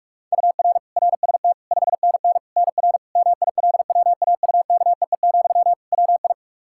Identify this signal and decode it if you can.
Morse code, heard as a keyed tone, about 46 wpm, 700 Hz. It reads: UR RST 5NN NR MILWAUKEE, WI
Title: Practice QSO - Exchange 1 - Beginning - 45wpm